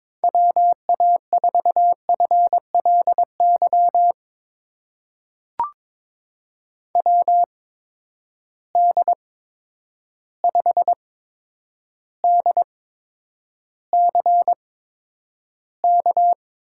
Morse code, heard as a keyed tone, about 22 wpm, spaced as for 9 wpm, 700 Hz. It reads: WA4FLY WD5DCK
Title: USA Call Signs - 6 Characters - 8x Character Spacing - 22wpm